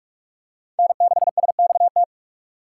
Morse code, heard as a keyed tone, about 44 wpm, 700 Hz. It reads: N6SXT